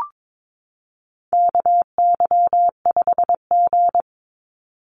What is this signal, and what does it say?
Morse code, heard as a keyed tone, about 22 wpm, 700 Hz. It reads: KY5G
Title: USA Call Signs 22wpm